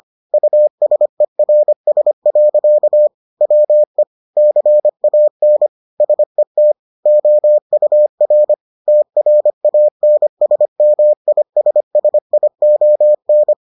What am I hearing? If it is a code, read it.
Morse code, heard as a keyed tone, about 25 wpm, 600 Hz. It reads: USERS. WE CAN SET OUR TRANSMISSION